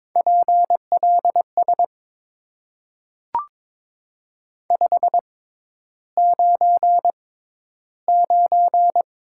Morse code, heard as a keyed tone, about 22 wpm, spaced as for 11 wpm, 700 Hz. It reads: PLS 599
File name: Top 90 QSO Elements - 6x Character Spacing - 22wpm